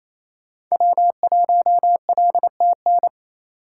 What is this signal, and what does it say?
Morse code, heard as a keyed tone, about 36 wpm, 700 Hz. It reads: W1LTN